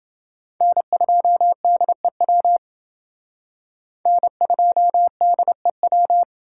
Morse code, heard as a keyed tone, about 28 wpm, 700 Hz. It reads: N2DEW N2DEW